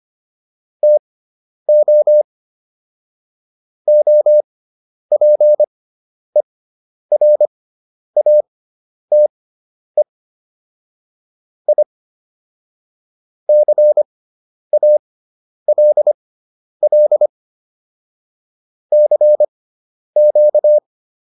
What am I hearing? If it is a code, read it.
Morse code, heard as a keyed tone, about 25 wpm, spaced as for 10 wpm, 600 Hz. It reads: TO OPERATE I CALL CQ